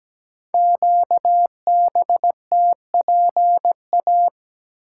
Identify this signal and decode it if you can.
Morse code, heard as a keyed tone, about 17 wpm, 700 Hz. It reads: QBTPA
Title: Code Groups - 5 Letters - 17wpm